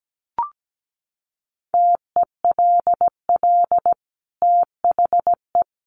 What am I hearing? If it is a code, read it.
Morse code, heard as a keyed tone, about 17 wpm, 700 Hz. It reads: TELL THE
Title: Sentences from Top 800 Words 17wpm